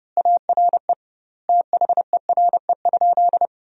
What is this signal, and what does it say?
Morse code, heard as a keyed tone, about 32 wpm, 700 Hz. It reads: ARE THERE?
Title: 2x Word-Spacing - Sentences from Top 100 Words - 30wpm